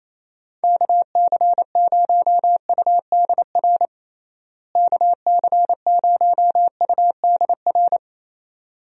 Morse code, heard as a keyed tone, about 27 wpm, 700 Hz. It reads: KC0UDR KC0UDR
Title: USA Call Signs - 6 Characters - Repeated 2x - 28wpm